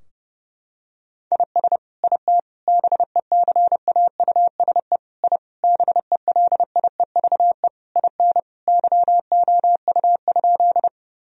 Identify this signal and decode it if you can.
Morse code, heard as a keyed tone, about 32 wpm, 700 Hz. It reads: IS IT BECAUSE I BELIEVE IN YOU?
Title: Sentences from Top 1000 Words - Review Part 3 - 30wpm